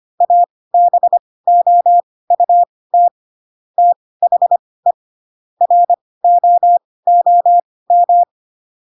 Morse code, heard as a keyed tone, about 25 wpm, spaced as for 18 wpm, 700 Hz. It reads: ABOUT THE ROOM
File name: Pinocchio / Pinocchio ch1 12 25 ik1hge